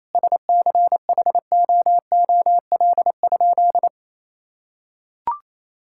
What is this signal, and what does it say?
Morse code, heard as a keyed tone, about 28 wpm, 700 Hz. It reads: SCHOOL?
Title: Sentences from Top 1000 Words - Review Part 6 - 28wpm